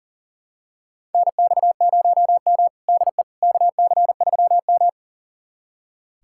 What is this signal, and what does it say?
Morse code, heard as a keyed tone, about 40 wpm, 700 Hz. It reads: NX0M DE KC3M
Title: USA Callsign DE Callsign - 4 Characters - 40wpm